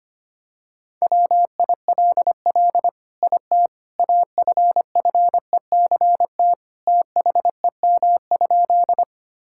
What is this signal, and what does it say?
Morse code, heard as a keyed tone, about 25 wpm, 700 Hz. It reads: WILL IT AFFECT THEM?
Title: Sentences from Top 1000 Words - Review Part 6 - 25wpm